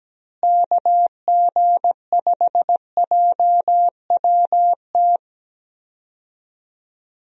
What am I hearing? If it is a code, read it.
Morse code, heard as a keyed tone, about 17 wpm, 700 Hz. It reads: KG5JWT
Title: USA Call Signs - 6 Characters - Repeated 3x - 17wpm